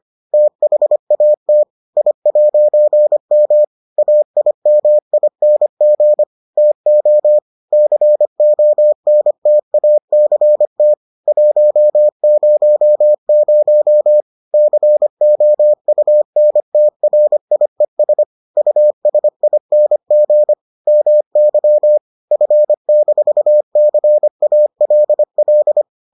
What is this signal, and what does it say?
Morse code, heard as a keyed tone, about 25 wpm, 600 Hz. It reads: THAT I'M AIMING TO CONTACT 100 COUNTRIES USING MY F-CALL